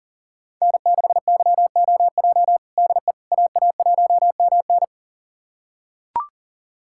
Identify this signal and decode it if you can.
Morse code, heard as a keyed tone, about 40 wpm, 700 Hz. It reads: N6YOJ DE AA1MN
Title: USA Callsign DE Callsign - 5 Characters - 40wpm